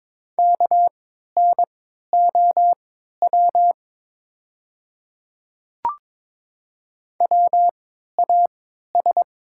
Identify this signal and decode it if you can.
Morse code, heard as a keyed tone, about 22 wpm, spaced as for 13 wpm, 700 Hz. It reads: KNOW WAS
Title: 3x Character Spacing - Top 200 Words - 22wpm